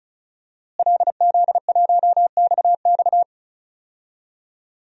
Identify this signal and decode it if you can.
Morse code, heard as a keyed tone, about 36 wpm, 700 Hz. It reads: LZ1XX
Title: DXCC-100 Call Signs - 5 Characters - Repeated 2x - 35wpm